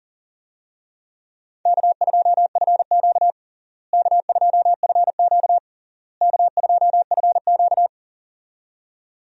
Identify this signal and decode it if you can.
Morse code, heard as a keyed tone, about 40 wpm, 700 Hz. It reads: K2FQ K2FQ K2FQ
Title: USA Call Signs - 4 Characters - Repeated 3x - 40wpm